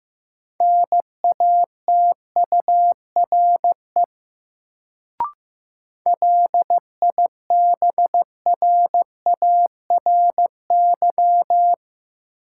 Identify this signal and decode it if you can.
Morse code, heard as a keyed tone, about 15 wpm, 700 Hz. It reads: NATURE LIBRARY
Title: Sentences from Top 600 Words - New Word Drill - 15wpm